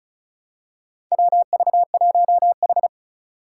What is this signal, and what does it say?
Morse code, heard as a keyed tone, about 36 wpm, 690 Hz. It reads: WV1H
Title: Four Challenging Letter-Number Combinations 35wpm